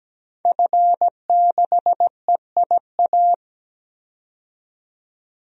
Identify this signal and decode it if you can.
Morse code, heard as a keyed tone, about 17 wpm, 700 Hz. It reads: F6EIA
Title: DXCC-100 Call Signs - 5 Characters - 17wpm